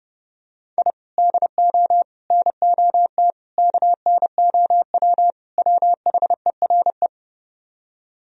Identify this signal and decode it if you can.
Morse code, heard as a keyed tone, about 28 wpm, 700 Hz. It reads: I DO NOT KNOW WHERE